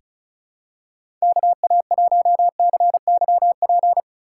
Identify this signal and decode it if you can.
Morse code, heard as a keyed tone, about 36 wpm, 700 Hz. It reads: KA1CYP
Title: USA Call Signs - 6 Characters - 6x Character Spacing - 35wpm